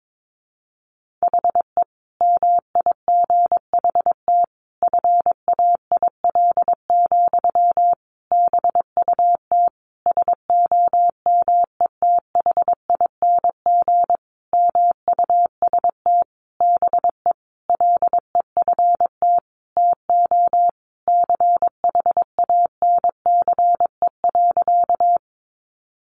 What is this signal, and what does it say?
Morse code, heard as a keyed tone, about 22 wpm, 700 Hz. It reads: HE MIGHT FAIL, BUT SOMETHING MUST BE LEFT TO CHANCE.
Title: Sentences from Top 1100 Words 22wpm